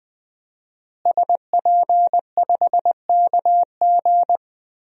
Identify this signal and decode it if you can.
Morse code, heard as a keyed tone, about 20 wpm, 700 Hz. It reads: SP5KG
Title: DXCC-100 Call Signs - 5 Characters - Repeated 2x - 20wpm